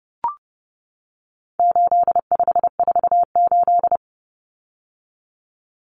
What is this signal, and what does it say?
Morse code, heard as a keyed tone, about 30 wpm, 700 Hz. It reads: 8548
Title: Numbers - 4-Digit 30wpm